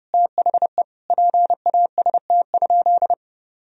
Morse code, heard as a keyed tone, about 30 wpm, 700 Hz. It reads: THE PAST?
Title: Sentences from Top 1000 Words - Review Part 4 - 30wpm